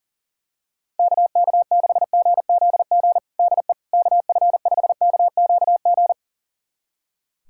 Morse code, heard as a keyed tone, about 40 wpm, 700 Hz. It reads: KK6GZG DE KF5KQG
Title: USA Callsign DE Callsign - 6 Characters - 40wpm